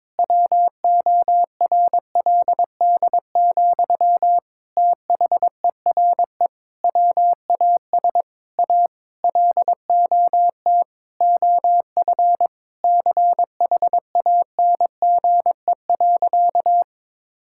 Morse code, heard as a keyed tone, about 21 wpm, 700 Hz. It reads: WORLD, THERE WAS A LOT OF CHANGE.
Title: Sentences from Top 1000 Words - Review Part 5 - 22wpm